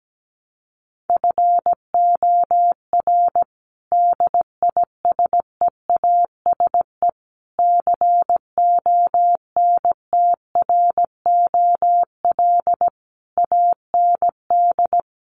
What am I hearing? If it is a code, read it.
Morse code, heard as a keyed tone, about 17 wpm, 700 Hz. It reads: FOR DISEASE CONTROL AND
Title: Sets of 5 Words Encore2 17wpm